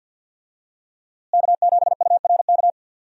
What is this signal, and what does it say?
Morse code, heard as a keyed tone, about 47 wpm, 700 Hz. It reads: K7URK
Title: USA Call Signs - 5 Characters - Repeated 2x - 50wpm